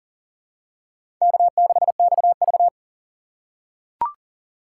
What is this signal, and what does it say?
Morse code, heard as a keyed tone, about 40 wpm, 700 Hz. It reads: K6XV
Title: USA Call Signs - 4 Characters - 6x Character Spacing - 40wpm